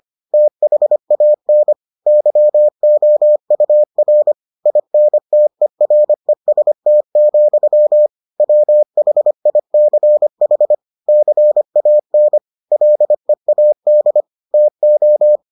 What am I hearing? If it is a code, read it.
Morse code, heard as a keyed tone, about 25 wpm, 600 Hz. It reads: THAN YOUR INTEREST, WHICH CAN LEAD TO